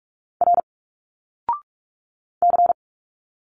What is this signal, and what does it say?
Morse code, heard as a keyed tone, about 46 wpm, 700 Hz. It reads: R C
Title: Single Letters 45wpm